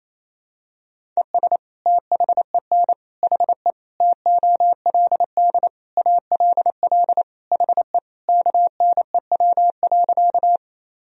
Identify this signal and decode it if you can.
Morse code, heard as a keyed tone, about 27 wpm, 700 Hz. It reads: ES THEN HE TOLD ALL HE KNEW.
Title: Sentences from Top 400 Words and Abbreviations 28wpm